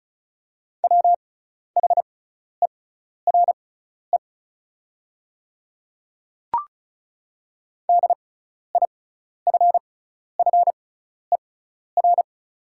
Morse code, heard as a keyed tone, about 36 wpm, spaced as for 12 wpm, 690 Hz. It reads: WHERE DIFFER
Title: 6x Character Spacing - Top 500 Words - 35wpm